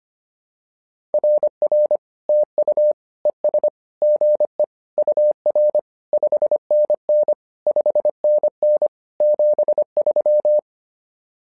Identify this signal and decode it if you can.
Morse code, heard as a keyed tone, about 25 wpm, 600 Hz. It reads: RR TU ES GE UR 5NN 5NN 73
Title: SOTA QSO Practice 25wpm